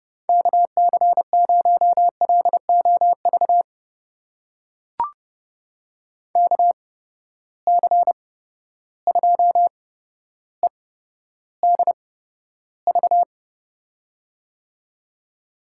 Morse code, heard as a keyed tone, about 30 wpm, spaced as for 13 wpm, 700 Hz. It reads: KC0LOV KC2EDV
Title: USA Call Signs - 6 Characters - 8x Character Spacing - 30wpm